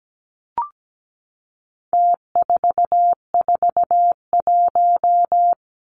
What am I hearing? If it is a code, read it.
Morse code, heard as a keyed tone, about 17 wpm, 700 Hz. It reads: T441
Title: Numbers - Contest Style - 4-Digit 17wpm